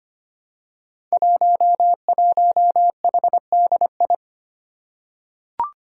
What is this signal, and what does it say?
Morse code, heard as a keyed tone, about 25 wpm, 700 Hz. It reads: 11HDI
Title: Five Challenging Letter-Number Combinations 25wpm